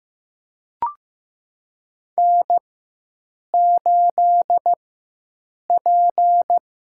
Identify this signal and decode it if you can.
Morse code, heard as a keyed tone, about 15 wpm, spaced as for 7 wpm, 700 Hz. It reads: N8P
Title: USA Call Signs - 5 Characters - 4x Character Spacing - 15wpm